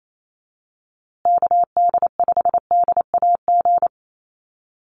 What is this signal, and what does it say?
Morse code, heard as a keyed tone, about 28 wpm, 700 Hz. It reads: KD5DAG
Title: USA Call Signs Repeated 2x 28wpm